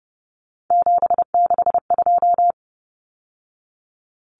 Morse code, heard as a keyed tone, about 30 wpm, 700 Hz. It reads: 762
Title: Numbers - 3-Digit 30wpm